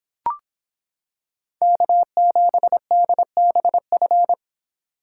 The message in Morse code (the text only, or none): K7DBF